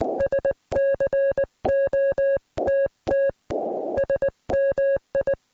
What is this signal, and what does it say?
Morse code, heard as a keyed tone, about 24 wpm, spaced as for 19 wpm, 590 Hz, with noise. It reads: SCOTT SMI